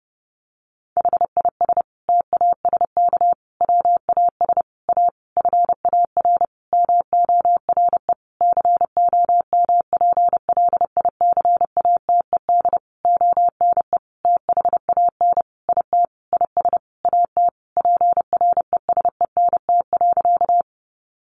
Morse code, heard as a keyed tone, about 30 wpm, 700 Hz. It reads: HIS TASK WAS A FAR MORE COMPLICATED ONE THAN IT IS AT PRESENT.
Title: Sentences from Top 1200 Words 30wpm